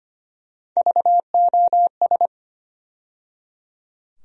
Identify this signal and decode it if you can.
Morse code, heard as a keyed tone, about 25 wpm, 700 Hz. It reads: VOS
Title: USA Call Sign Postfixes - 3 Letters 25wpm